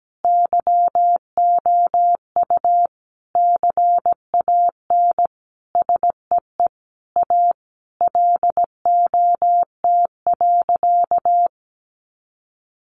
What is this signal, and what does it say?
Morse code, heard as a keyed tone, about 17 wpm, 700 Hz. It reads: YOU CAN SEE A LOT.
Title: Sentences from Top 1500 Words 17wpm